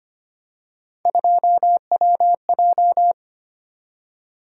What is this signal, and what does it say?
Morse code, heard as a keyed tone, about 25 wpm, 700 Hz. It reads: 2WJ
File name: Three Letter-Number Combinations - 25wpm